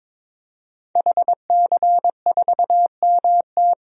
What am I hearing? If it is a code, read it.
Morse code, heard as a keyed tone, about 22 wpm, 700 Hz. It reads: HC4MT
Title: DXCC-100 Call Signs - 5 Characters - 22wpm